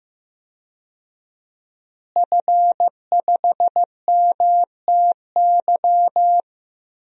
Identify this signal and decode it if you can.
Morse code, heard as a keyed tone, about 15 wpm, 700 Hz. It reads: F5MTY
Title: DXCC-100 Call Signs - 5 Characters - Repeated 3x - 15wpm